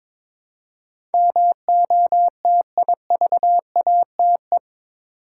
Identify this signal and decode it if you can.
Morse code, heard as a keyed tone, about 22 wpm, 700 Hz. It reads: MOTIVATE
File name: practice / 22wpm / Eight Letter Words - 22wpm